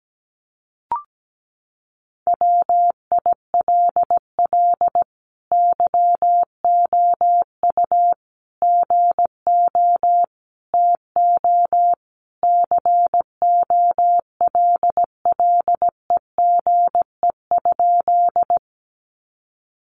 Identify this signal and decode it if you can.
Morse code, heard as a keyed tone, about 17 wpm, 700 Hz. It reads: WILL YOU GO TO COLLEGE?
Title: Sentences from Top 1300 Words 17wpm